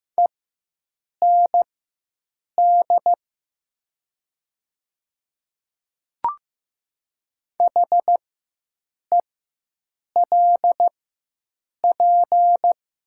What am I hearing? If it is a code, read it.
Morse code, heard as a keyed tone, about 15 wpm, spaced as for 7 wpm, 700 Hz. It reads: END HELP
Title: 4x Character Spacing - Top 200 Words - 15wpm